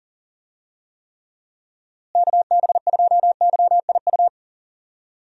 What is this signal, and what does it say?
Morse code, heard as a keyed tone, about 40 wpm, 700 Hz. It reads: KB2YIU
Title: USA Call Signs - 6 Characters - Repeated 3x - 40wpm